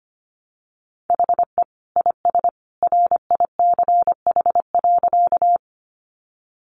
Morse code, heard as a keyed tone, about 25 wpm, 700 Hz. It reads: HE IS RICH.
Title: Sentences from Top 1100 Words 25wpm